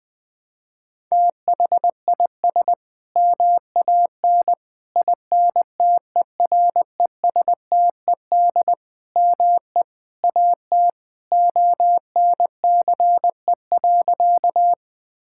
Morse code, heard as a keyed tone, about 20 wpm, 700 Hz. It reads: THIS MAN INTERESTED ME AT ONCE.
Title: Sentences from Top 1000 Words - Review Part 6 - 20wpm